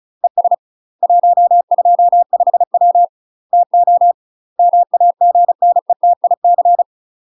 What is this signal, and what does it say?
Morse code, heard as a keyed tone, about 37 wpm, 700 Hz. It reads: ES 125W TO MAGNETIC
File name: UK MASTER QSOs TRAINING / 35WPM / UK MASTER QSOs 0019